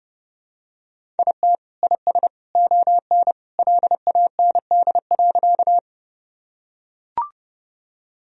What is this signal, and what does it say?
Morse code, heard as a keyed tone, about 32 wpm, 700 Hz. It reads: IT IS ON LAND.
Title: Sentences from Top 1000 Words - Review Part 1 - 30wpm